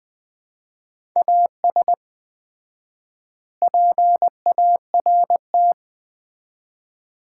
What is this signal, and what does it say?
Morse code, heard as a keyed tone, about 20 wpm, 700 Hz. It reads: AS PART
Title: 4x Word-Spacing - Sets of 4 Words - 20wpm